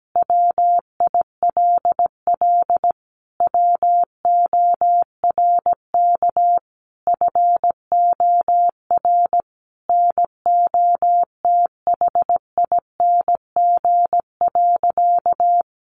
Morse code, heard as a keyed tone, about 17 wpm, 700 Hz. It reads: WILL WORK FOR NOTHING.
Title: Sentences from Top 1400 Words 17wpm